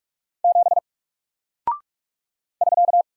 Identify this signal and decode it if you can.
Morse code, heard as a keyed tone, about 44 wpm, 700 Hz. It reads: 7 <SK>
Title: Single Character (Letter:Number:Punctuation:Prosign) Alt 45wpm